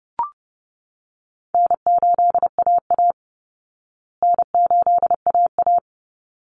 Morse code, heard as a keyed tone, about 30 wpm, 700 Hz. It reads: N8AA N8AA
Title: Call Signs Repeated 3x - 30wpm